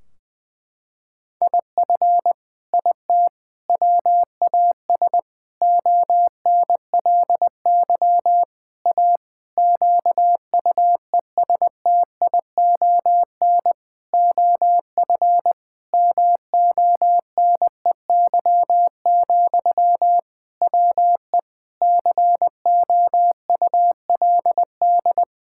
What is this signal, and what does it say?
Morse code, heard as a keyed tone, about 20 wpm, 700 Hz. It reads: IF IT WAS ONLY A QUESTION OF MONEY, WE COULD
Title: Sentences from Top 1600 Words 20wpm